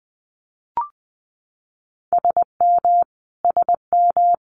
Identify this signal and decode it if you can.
Morse code, heard as a keyed tone, about 20 wpm, 700 Hz. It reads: SM SM
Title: Multi-letter Phonemes Repeated 3x 20wpm